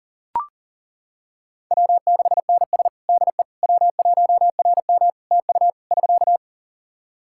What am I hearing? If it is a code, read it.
Morse code, heard as a keyed tone, about 37 wpm, 700 Hz. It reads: W6NS DE W1RM TU <SK>
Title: Practice QSO - Exchange 4 - Ending - 40wpm